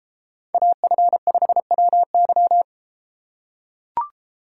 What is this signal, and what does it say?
Morse code, heard as a keyed tone, about 30 wpm, 700 Hz. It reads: AF5WY